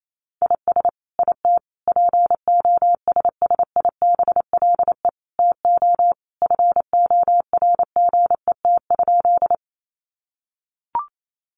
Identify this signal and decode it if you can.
Morse code, heard as a keyed tone, about 28 wpm, 700 Hz. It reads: IS IT POSSIBLE TO FORGET?
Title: Sentences from Top 1200 Words 28wpm